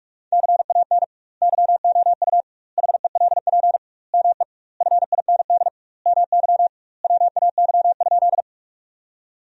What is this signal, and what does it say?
Morse code, heard as a keyed tone, about 42 wpm, 700 Hz. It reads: CAN YOU HELP ME FIND MY WAY?
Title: 2x Word-Spacing - Sentences from Top 200 Words - 45wpm